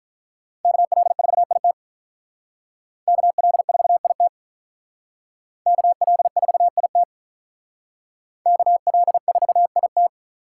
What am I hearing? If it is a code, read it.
Morse code, heard as a keyed tone, about 44 wpm, 700 Hz. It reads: KL4IT KL4IT KL4IT KL4IT